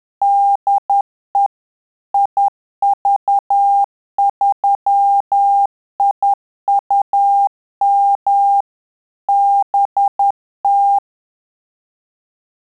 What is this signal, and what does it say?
Morse code, heard as a keyed tone, about 10 wpm, 800 Hz. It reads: DE IV3IUM BT
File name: Telegraphy-group / QSO CW 50 CPM